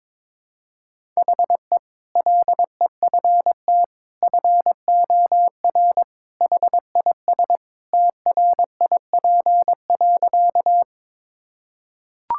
Morse code, heard as a keyed tone, about 22 wpm, 700 Hz. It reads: HE LEFT FOR HIS TRIP.
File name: practice / 22wpm / Sentences with 2-Word Phrases - 22wpm